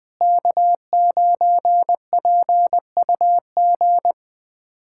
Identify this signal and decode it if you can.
Morse code, heard as a keyed tone, about 20 wpm, 700 Hz. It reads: K9PUG
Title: USA Callsign DE Callsign 20wpm